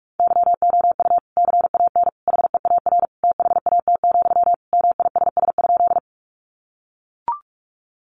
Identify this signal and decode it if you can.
Morse code, heard as a keyed tone, about 44 wpm, 700 Hz. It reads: YOU CAN HEAR THAT, MISS?
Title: Sentences from Top 500 Words 45wpm